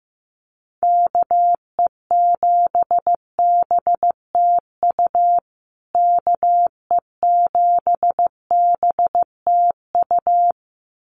Morse code, heard as a keyed tone, about 15 wpm, 700 Hz. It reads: KE7BTU KE7BTU
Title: USA Call Signs Repeated 2x 15wpm